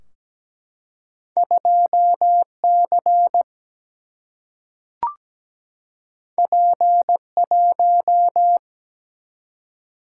Morse code, heard as a keyed tone, about 17 wpm, 700 Hz. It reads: 2C P1
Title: Single-Letter Single-Number 17wpm